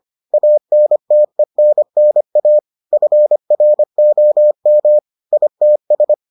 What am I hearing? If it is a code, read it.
Morse code, heard as a keyed tone, about 25 wpm, 600 Hz. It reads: ANTENNA FROM ITS